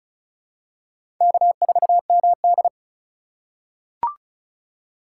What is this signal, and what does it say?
Morse code, heard as a keyed tone, about 34 wpm, 700 Hz. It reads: K4MD
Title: USA Call Signs - 2x Character Spacing - 35wpm